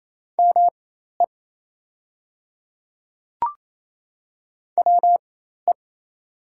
Morse code, heard as a keyed tone, about 28 wpm, spaced as for 13 wpm, 700 Hz. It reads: ME WE
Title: 4x Character Spacing - Top 200 Words - 28wpm